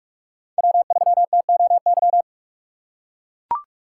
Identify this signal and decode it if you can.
Morse code, heard as a keyed tone, about 44 wpm, 700 Hz. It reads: W3TOY
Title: USA Call Signs - 5 Characters - 6x Character Spacing - 45wpm